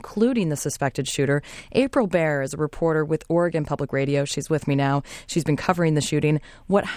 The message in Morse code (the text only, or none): none